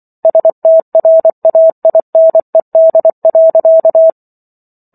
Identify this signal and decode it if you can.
Morse code, heard as a keyed tone, about 24 wpm, 650 Hz. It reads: STRAINED.